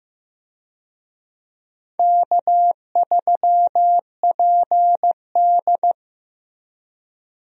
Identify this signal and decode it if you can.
Morse code, heard as a keyed tone, about 15 wpm, 700 Hz. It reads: K3PD
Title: USA Call Signs - 4 Characters - Repeated 3x - 15wpm